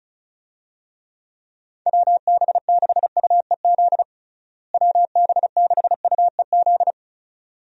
Morse code, heard as a keyed tone, about 35 wpm, 700 Hz. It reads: WB6UEZ WB6UEZ
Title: USA Call Signs - 6 Characters - Repeated 2x - 35wpm